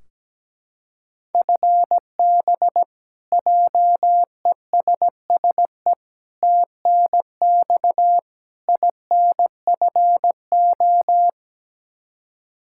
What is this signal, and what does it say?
Morse code, heard as a keyed tone, about 17 wpm, 700 Hz. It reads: FB JESSE TNX INFO
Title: Practice QSO - Exchange 3 - More Info - 17wpm